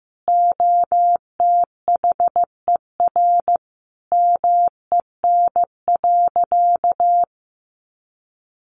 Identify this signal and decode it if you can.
Morse code, heard as a keyed tone, about 15 wpm, 700 Hz. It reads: OTHER MEN.
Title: Sentences from Top 1100 Words 15wpm